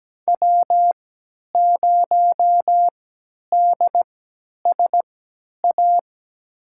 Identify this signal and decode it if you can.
Morse code, heard as a keyed tone, about 17 wpm, spaced as for 10 wpm, 700 Hz. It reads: W0DSA
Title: USA Call Signs - 3x Character Spacing - 17wpm